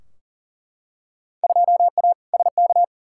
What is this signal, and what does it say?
Morse code, heard as a keyed tone, about 40 wpm, 700 Hz. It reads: 2A SK
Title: Field Day Exchanges Encore - 40wpm